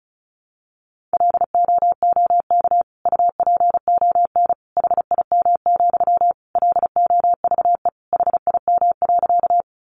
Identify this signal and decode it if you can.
Morse code, heard as a keyed tone, about 34 wpm, 700 Hz. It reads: LOOK UPON HIM, LOVE HIM.
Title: Sentences from Top 1100 Words 35wpm